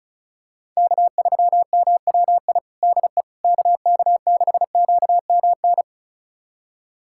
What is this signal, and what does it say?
Morse code, heard as a keyed tone, about 34 wpm, 700 Hz. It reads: K3MWI DE KK6QMN
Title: USA Callsign DE Callsign 35wpm